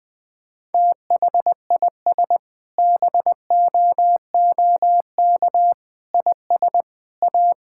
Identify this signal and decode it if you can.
Morse code, heard as a keyed tone, about 20 wpm, 700 Hz. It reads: THIS BOOK IS A
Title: Sentences with 6-Word Phrases - 20wpm